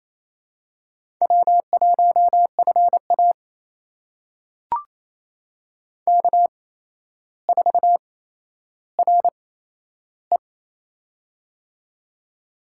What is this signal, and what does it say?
Morse code, heard as a keyed tone, about 27 wpm, spaced as for 12 wpm, 700 Hz. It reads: W1FA K4RE
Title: USA Call Signs - 4 Characters - 8x Character Spacing - 28wpm